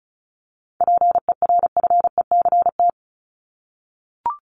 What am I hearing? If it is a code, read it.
Morse code, heard as a keyed tone, about 34 wpm, 700 Hz. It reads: PERFECT
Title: Top 751-1000 Words 35wpm